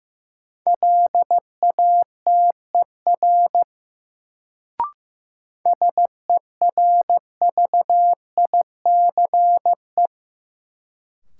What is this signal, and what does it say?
Morse code, heard as a keyed tone, about 15 wpm, 700 Hz. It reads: LATER SERVICE
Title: Sentences from Top 700 Words - New Word Drill - 15wpm